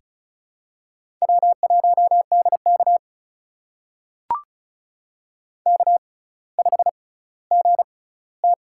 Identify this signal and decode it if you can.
Morse code, heard as a keyed tone, about 36 wpm, spaced as for 18 wpm, 700 Hz. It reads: W1DK K5GT
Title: USA Call Signs - 6x Character Spacing - 35wpm